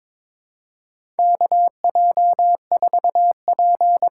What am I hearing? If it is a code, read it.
Morse code, heard as a keyed tone, about 22 wpm, 700 Hz. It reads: KJ4P
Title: USA Call Signs - 8x Character Spacing - 22wpm